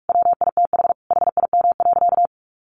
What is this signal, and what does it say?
Morse code, heard as a keyed tone, about 44 wpm, 700 Hz. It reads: WITH HIM.
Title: Sentences from Top 1400 Words 45wpm